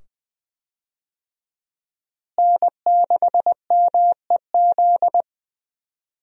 Morse code, heard as a keyed tone, about 20 wpm, 700 Hz. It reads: N6MEZ
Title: USA Call Signs - 5 Characters - Repeated 2x - 20wpm